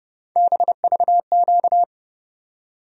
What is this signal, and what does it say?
Morse code, heard as a keyed tone, about 30 wpm, 700 Hz. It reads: BVQ
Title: USA Call Sign Postfixes - 3 Letters 30wpm